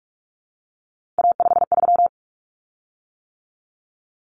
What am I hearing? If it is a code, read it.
Morse code, heard as a keyed tone, about 44 wpm, 700 Hz. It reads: A53
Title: Common Cut Numbers - 3-Digit 45wpm